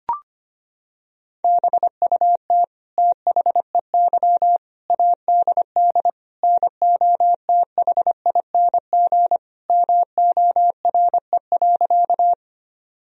Morse code, heard as a keyed tone, about 25 wpm, 700 Hz. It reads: BUT THEY ADD NOTHING MORE.
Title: Sentences from Top 1000 Words - Review Part 1 - 25wpm